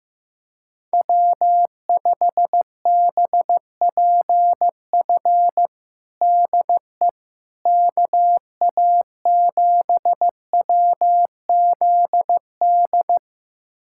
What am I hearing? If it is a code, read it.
Morse code, heard as a keyed tone, about 15 wpm, 700 Hz. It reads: W5BPF DE KA7WZD
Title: USA Callsign DE Callsign 15wpm